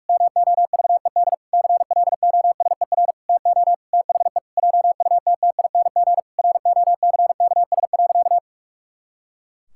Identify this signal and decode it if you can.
Morse code, heard as a keyed tone, about 41 wpm, 700 Hz. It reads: MOVED CLOSER TO THE JUTTING ROCKS.
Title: Pangram Challenge v2 - 45wpm